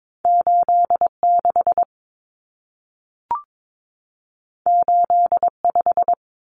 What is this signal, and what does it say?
Morse code, heard as a keyed tone, about 22 wpm, 700 Hz. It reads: 86 85
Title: Numbers - 2-Digit 22wpm